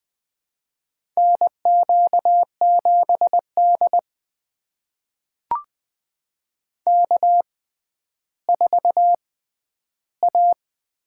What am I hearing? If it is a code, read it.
Morse code, heard as a keyed tone, about 21 wpm, spaced as for 11 wpm, 700 Hz. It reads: NQ7D K4A
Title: USA Call Signs - 4 Characters - 6x Character Spacing - 20wpm